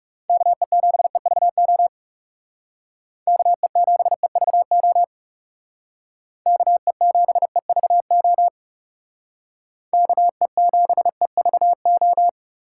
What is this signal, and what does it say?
Morse code, heard as a keyed tone, about 38 wpm, 700 Hz. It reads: KE7EVO KE7EVO KE7EVO KE7EVO